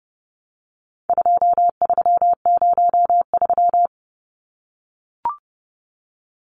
Morse code, heard as a keyed tone, about 30 wpm, 700 Hz. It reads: 2303
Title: Numbers - 4-Digit 30wpm